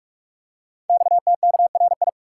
Morse code, heard as a keyed tone, about 44 wpm, 700 Hz. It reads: XTKRI
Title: Code Groups - 5 Letters - 45wpm